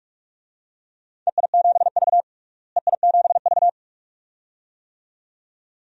Morse code, heard as a keyed tone, about 46 wpm, 700 Hz. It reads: EI7V EI7V
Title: DXCC-100 Call Signs - 4 Characters - Repeated 2x - 45wpm